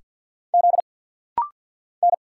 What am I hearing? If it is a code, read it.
Morse code, heard as a keyed tone, about 48 wpm, 700 Hz. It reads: Z N
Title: Single Letters - Rapid-Fire 50wpm